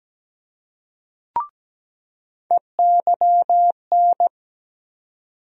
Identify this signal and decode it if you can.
Morse code, heard as a keyed tone, about 17 wpm, 700 Hz. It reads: EYN
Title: USA Call Sign Postfixes - 3 Letters 17wpm